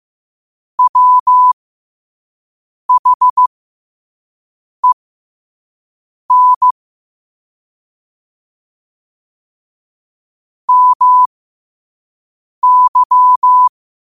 Morse code, heard as a keyed tone, about 15 wpm, spaced as for 5 wpm, 1000 Hz. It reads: WHEN MY
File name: qotdfiles / 5wpm